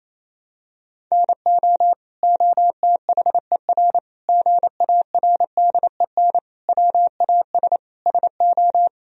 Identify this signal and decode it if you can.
Morse code, heard as a keyed tone, about 28 wpm, 700 Hz. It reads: NO OTHER GARDEN WAS SO
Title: Sentences from Top 1000 Words - Review Part 4 - 28wpm